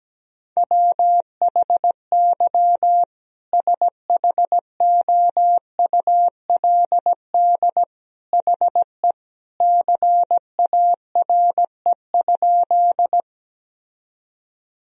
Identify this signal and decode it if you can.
Morse code, heard as a keyed tone, about 17 wpm, 700 Hz. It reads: WHY SHOULD HE CARE?
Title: Sentences from Top 1000 Words - Review Part 1 - 17wpm